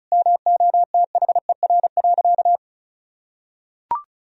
Morse code, heard as a keyed tone, about 36 wpm, 700 Hz. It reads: MOTHER.
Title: Sentences from Top 1000 Words - Review Part 4 - 35wpm